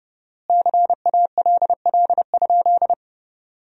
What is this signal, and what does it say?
Morse code, heard as a keyed tone, about 30 wpm, 700 Hz. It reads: CALL?
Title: 4x Word-Spacing - Sentences from Top 100 Words - 30wpm